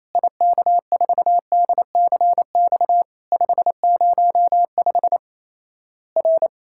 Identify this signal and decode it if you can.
Morse code, heard as a keyed tone, about 28 wpm, 700 Hz. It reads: IK4DCX 505 R
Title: Worked All Europe DX Contest - QTC Exchange - 28wpm